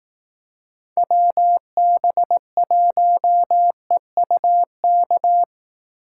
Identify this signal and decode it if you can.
Morse code, heard as a keyed tone, about 18 wpm, 700 Hz. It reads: WB1EUK
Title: Speed Racing - USA 2x3 Call Signs 15wpm